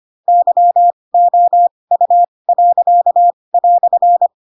Morse code, heard as a keyed tone, about 25 wpm, spaced as for 20 wpm, 700 Hz. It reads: YOU."
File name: Pinocchio / Pinocchio ch4 15 25 ik1hge